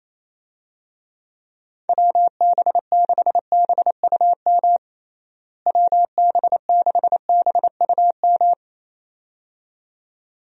Morse code, heard as a keyed tone, about 28 wpm, 700 Hz. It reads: WB6BUM WB6BUM
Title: USA Call Signs - 6 Characters - Repeated 2x - 28wpm